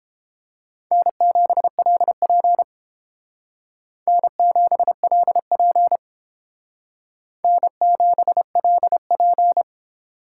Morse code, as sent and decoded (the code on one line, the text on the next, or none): -. --... .-.. .--. / -. --... .-.. .--. / -. --... .-.. .--.
N7LP N7LP N7LP